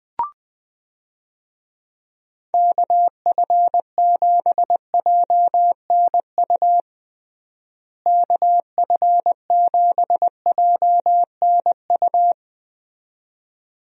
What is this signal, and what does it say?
Morse code, heard as a keyed tone, about 20 wpm, 700 Hz. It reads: KF7JNU KF7JNU
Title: USA Call Signs - 6 Characters - Repeated 2x - 20wpm